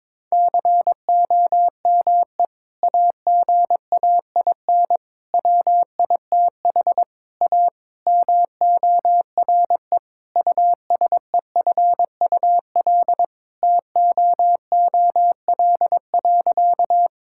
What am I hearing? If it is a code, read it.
Morse code, heard as a keyed tone, about 22 wpm, 700 Hz. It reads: COME AGAIN WITH A MORE USEFUL TOOL.
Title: Sentences from Top 1000 Words - Review Part 6 - 22wpm